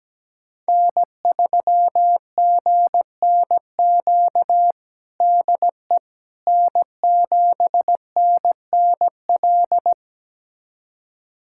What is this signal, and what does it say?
Morse code, heard as a keyed tone, about 17 wpm, 700 Hz. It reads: N3GNQ DE N7NNL
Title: USA Callsign DE Callsign - 5 Characters - 17wpm